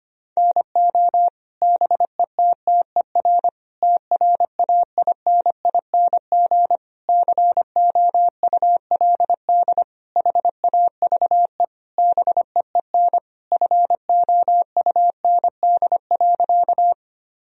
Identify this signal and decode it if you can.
Morse code, heard as a keyed tone, about 25 wpm, 700 Hz. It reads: NO BETTER TRAINING COULD HAVE BEEN FOUND.
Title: Sentences from Top 1000 Words - Review Part 4 - 25wpm